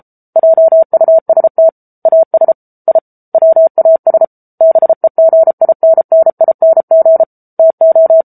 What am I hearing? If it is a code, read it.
Morse code, heard as a keyed tone, about 34 wpm, 650 Hz. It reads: JUST AS I WAS BEGINNING TO
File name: Book Two / 07-Chapter 7 - 33 WPM